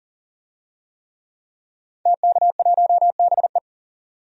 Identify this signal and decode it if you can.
Morse code, heard as a keyed tone, about 40 wpm, 700 Hz. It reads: TK1BE
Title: DXCC-100 Call Signs - 5 Characters - 40wpm